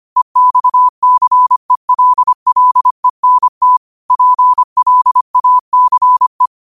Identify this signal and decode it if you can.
Morse code, heard as a keyed tone, about 25 wpm, 1000 Hz. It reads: EXCELLENT PLACE
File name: qotdfiles / 25wpm